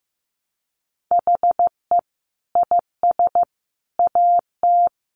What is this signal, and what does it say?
Morse code, heard as a keyed tone, about 15 wpm, 700 Hz. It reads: HE IS AT
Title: Sentences from Top 300 Words 15wpm